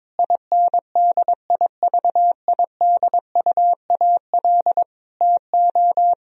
Morse code, heard as a keyed tone, about 22 wpm, 700 Hz. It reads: INDIVIDUAL TO
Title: Sentences from Top 1000 Words - Review Part 4 - 22wpm